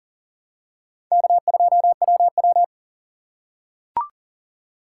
Morse code, heard as a keyed tone, about 40 wpm, 700 Hz. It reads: K2WW
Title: USA Call Signs - 4 Characters - 4x Character Spacing - 40wpm